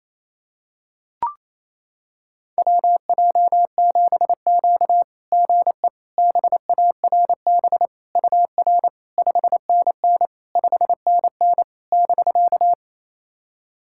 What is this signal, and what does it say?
Morse code, heard as a keyed tone, about 28 wpm, 700 Hz. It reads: WJ7Q GE BARB UR 5NN 5NN <BK>
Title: SOTA QSO Practice 28wpm